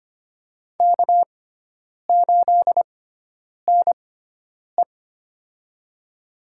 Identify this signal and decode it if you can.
Morse code, heard as a keyed tone, about 25 wpm, spaced as for 9 wpm, 700 Hz. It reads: K8NE
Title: USA Call Signs - 4 Characters - 6x Character Spacing - 25wpm